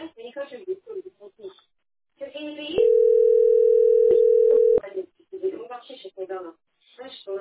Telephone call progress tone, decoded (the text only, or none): ringback tone